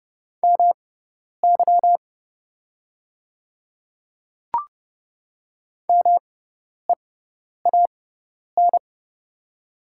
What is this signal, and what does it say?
Morse code, heard as a keyed tone, about 30 wpm, spaced as for 10 wpm, 690 Hz. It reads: MY MEAN
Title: 6x Character Spacing - Top 200 Words - 30wpm